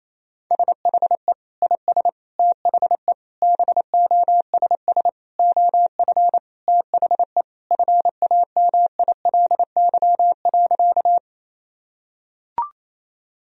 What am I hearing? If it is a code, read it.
Morse code, heard as a keyed tone, about 28 wpm, 700 Hz. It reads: SHE IS THE BOSS OF THE FAMILY.
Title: Sentences from Top 1000 Words - Review Part 4 - 28wpm